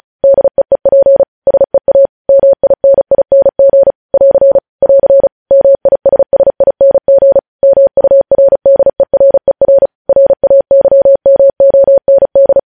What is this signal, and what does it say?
Morse code, heard as a keyed tone, about 36 wpm, 550 Hz. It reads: DEEP SEA MINING <AR> <AR> MISSING MURDERER RAYMOND
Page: News headlines in Morse code at 35 wpm